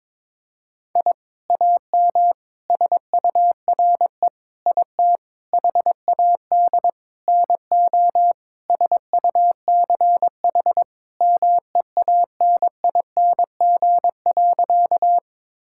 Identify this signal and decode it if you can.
Morse code, heard as a keyed tone, about 21 wpm, 700 Hz. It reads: I AM SURE IT HAD NO SUCH MEANING.
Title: Sentences from Top 1000 Words - Review Part 6 - 22wpm